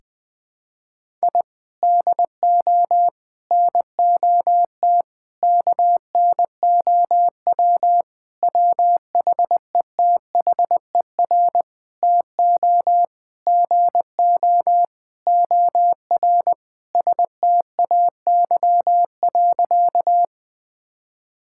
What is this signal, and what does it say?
Morse code, heard as a keyed tone, about 21 wpm, 700 Hz. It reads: I DO NOT KNOW WHETHER TO GO OR STAY.
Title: Sentences with 5-Word Phrases - 20wpm